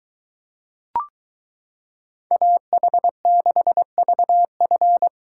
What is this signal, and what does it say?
Morse code, heard as a keyed tone, about 23 wpm, 700 Hz. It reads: AH6VF